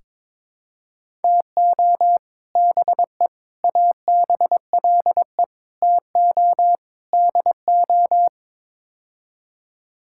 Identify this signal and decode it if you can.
Morse code, heard as a keyed tone, about 22 wpm, 700 Hz. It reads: TO BE ABLE TO DO
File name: practice / 22wpm / Build A Phrase - 5-Word Phrases - 22wpm